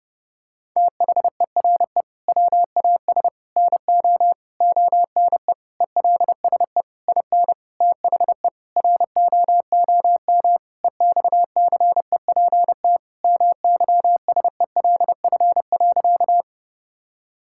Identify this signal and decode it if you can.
Morse code, heard as a keyed tone, about 30 wpm, 700 Hz. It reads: THERE WAS NO ONE ELSE IN THE ROOM EXCEPT MYSELF.
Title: Sentences from Top 1600 Words 30wpm